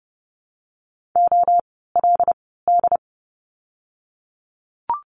Morse code, heard as a keyed tone, about 30 wpm, spaced as for 17 wpm, 700 Hz. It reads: OLD
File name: Lesson 17f - Letter u - Words 30wpm